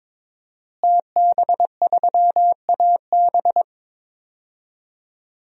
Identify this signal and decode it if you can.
Morse code, heard as a keyed tone, about 22 wpm, 700 Hz. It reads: TB3AB
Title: DXCC-100 Call Signs - 5 Characters - Repeated 3x - 22wpm